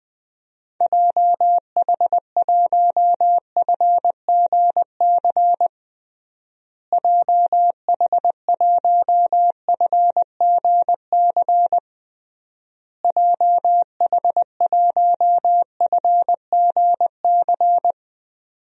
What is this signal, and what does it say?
Morse code, heard as a keyed tone, about 20 wpm, 700 Hz. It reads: JH1FGC JH1FGC JH1FGC
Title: DXCC-100 Call Signs - 6 Characters - Repeated 3x - 20wpm